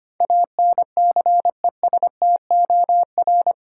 Morse code, heard as a keyed tone, about 25 wpm, 700 Hz. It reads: ANCESTOR
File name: 8 Letter Words - First 3 Letters Given - 25wpm